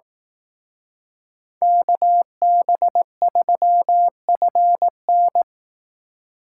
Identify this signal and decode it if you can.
Morse code, heard as a keyed tone, about 18 wpm, 700 Hz. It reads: KB3FN
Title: Speed Racing - USA 2x2 Call Signs 15wpm